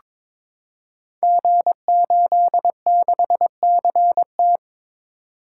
Morse code, heard as a keyed tone, about 22 wpm, 700 Hz. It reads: G86CT
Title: Code Groups - 5 Letter:Numbers - 22wpm